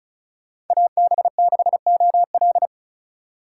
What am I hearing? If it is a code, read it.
Morse code, heard as a keyed tone, about 47 wpm, 700 Hz. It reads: AB6OL